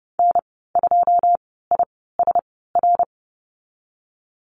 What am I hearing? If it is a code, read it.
Morse code, heard as a keyed tone, about 32 wpm, spaced as for 17 wpm, 700 Hz. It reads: N2ISR
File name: Lesson 19g - Letter w - Call Signs 30wpm